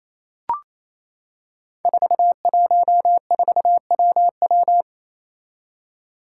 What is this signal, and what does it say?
Morse code, heard as a keyed tone, about 28 wpm, 700 Hz. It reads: 414WW
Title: Five Challenging Letter-Number Combinations 28wpm